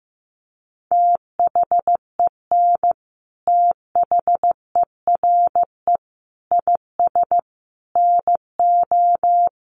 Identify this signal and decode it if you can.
Morse code, heard as a keyed tone, about 15 wpm, 700 Hz. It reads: THEN THERE IS NO
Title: Sentences from Top 700 Words 15wpm